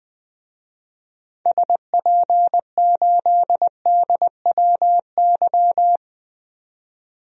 Morse code, heard as a keyed tone, about 20 wpm, 700 Hz. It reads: SP8DWY
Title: DXCC-100 Call Signs - 6 Characters - 20wpm